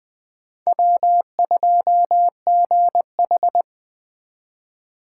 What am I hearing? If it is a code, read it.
Morse code, heard as a keyed tone, about 20 wpm, 700 Hz. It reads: W2GH
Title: Speed Racing - USA 1x2 Call Signs 17wpm